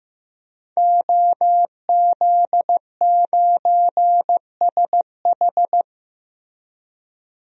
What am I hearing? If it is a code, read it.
Morse code, heard as a keyed tone, about 15 wpm, 700 Hz. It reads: OZ9SH
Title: DXCC-100 Call Signs - 5 Characters - 15wpm